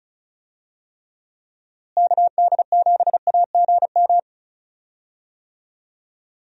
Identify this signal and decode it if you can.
Morse code, heard as a keyed tone, about 36 wpm, 700 Hz. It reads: KD7AGM
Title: USA Call Signs - 6 Characters - Repeated 2x - 35wpm